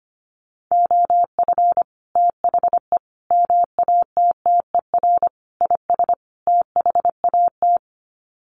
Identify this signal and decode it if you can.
Morse code, heard as a keyed tone, about 25 wpm, 700 Hz. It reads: OF THE MATTER IS THAT
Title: Sets of 5 Words Encore 25wpm